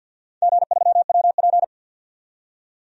Morse code, heard as a keyed tone, about 48 wpm, 700 Hz. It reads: G3WP
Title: DXCC-100 Call Signs - 4 Characters - 50wpm